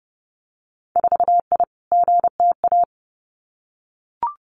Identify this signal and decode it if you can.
Morse code, heard as a keyed tone, about 30 wpm, 700 Hz. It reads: 4I GTA
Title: Winter Field Day Exchanges Encore 30wpm